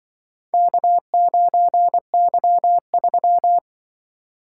Morse code, heard as a keyed tone, about 25 wpm, 700 Hz. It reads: K9Y3
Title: Speed-Racing - Four Letter-Number Combinations - 20wpm